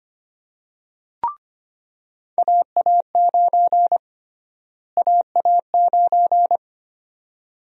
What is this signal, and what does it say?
Morse code, heard as a keyed tone, about 25 wpm, 700 Hz. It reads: AA9 AA9
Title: USA Call Sign Prefixes - Repeated 2x - 25wpm